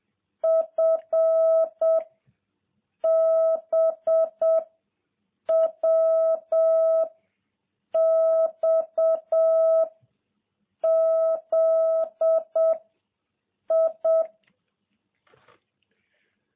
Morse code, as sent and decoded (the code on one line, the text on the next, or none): ..-. / -... / .-- / -..- / --.. / ..
F B W X Z I